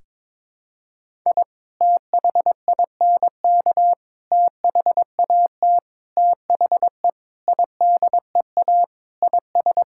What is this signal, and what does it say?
Morse code, heard as a keyed tone, about 22 wpm, 700 Hz. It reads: I THINK THAT THE IDEA IS
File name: practice / 22wpm / Sentences with 4-Word Phrases - 22wpm